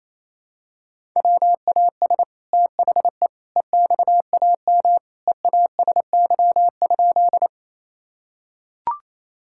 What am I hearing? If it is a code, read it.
Morse code, heard as a keyed tone, about 28 wpm, 700 Hz. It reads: WAS THE EXAM EASY?
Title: Sentences from Top 1000 Words - Review Part 5 - 28wpm